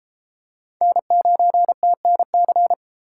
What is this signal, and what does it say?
Morse code, heard as a keyed tone, about 30 wpm, 700 Hz. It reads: N9TNC